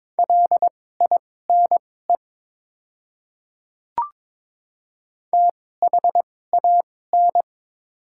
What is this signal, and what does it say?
Morse code, heard as a keyed tone, about 22 wpm, spaced as for 16 wpm, 700 Hz. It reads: LINE THAN
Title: 2x Character Spacing - Top 200 Words - 22wpm